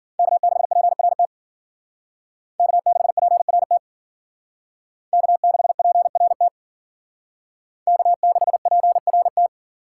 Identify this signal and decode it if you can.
Morse code, heard as a keyed tone, about 48 wpm, 700 Hz. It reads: K6PRT K6PRT K6PRT K6PRT
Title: Speed Racing - USA 1x3 Call Signs 40wpm